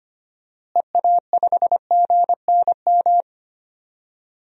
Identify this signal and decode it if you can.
Morse code, heard as a keyed tone, about 25 wpm, 700 Hz. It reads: EA5GNM
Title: DXCC-100 Call Signs - 6 Characters - Repeated 2x - 25wpm